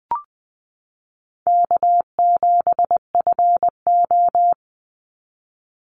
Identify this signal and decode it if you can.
Morse code, heard as a keyed tone, about 19 wpm, 700 Hz. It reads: K7FO
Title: NA SOTA Chaser Callsigns 20wpm